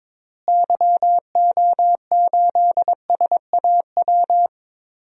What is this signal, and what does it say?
Morse code, heard as a keyed tone, about 22 wpm, 700 Hz. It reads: YO8SAW
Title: DXCC-100 Call Signs - 6 Characters - Repeated 3x - 22wpm